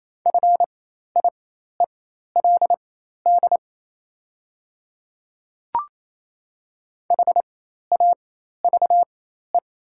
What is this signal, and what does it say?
Morse code, heard as a keyed tone, about 28 wpm, spaced as for 13 wpm, 690 Hz. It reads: FIELD HAVE
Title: 4x Character Spacing - Top 500 Words - 28wpm